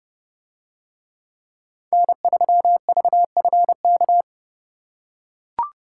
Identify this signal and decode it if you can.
Morse code, heard as a keyed tone, about 28 wpm, 700 Hz. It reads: N3VFK